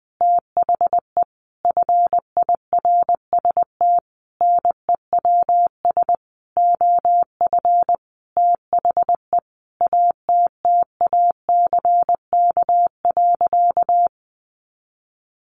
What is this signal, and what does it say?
Morse code, heard as a keyed tone, about 20 wpm, 700 Hz. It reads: THE FIRST NEWS OF THE ATTACK.
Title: Sentences from Top 1500 Words 20wpm